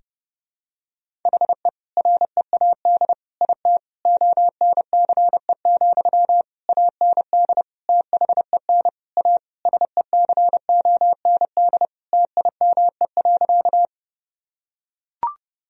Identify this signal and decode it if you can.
Morse code, heard as a keyed tone, about 32 wpm, 700 Hz. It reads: HE READ IT ONCE, AND THEN A SECOND TIME.
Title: Sentences from Top 1000 Words - Review Part 1 - 30wpm